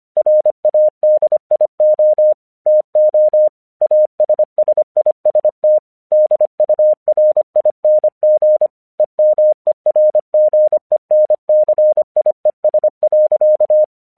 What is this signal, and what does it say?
Morse code, heard as a keyed tone, about 24 wpm, 610 Hz. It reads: RADIO TO ASSIST DURING EMERGENCIES.
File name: 1 English 25wpm / AQC06